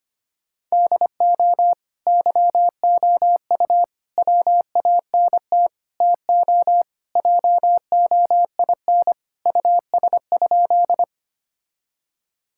Morse code, heard as a keyed tone, about 25 wpm, 700 Hz. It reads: DO YOU WANT TO JOIN US?
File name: practice / 25wpm / Sentences with 4-Word Phrases - 25wpm